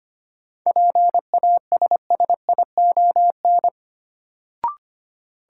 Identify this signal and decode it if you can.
Morse code, heard as a keyed tone, about 25 wpm, 700 Hz. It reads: PASSION
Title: Sentences from Top 1300 Words - New Word Drill - 25wpm